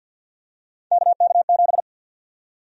Morse code, heard as a keyed tone, about 52 wpm, 700 Hz. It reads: KK7
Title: USA Call Sign Prefixes - 50wpm